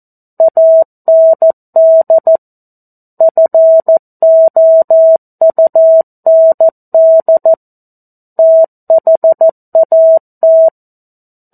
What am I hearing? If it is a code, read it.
Morse code, heard as a keyed tone, about 14 wpm, 650 Hz. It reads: AND FOUND THAT
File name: 02-Chapter-02 14 WPM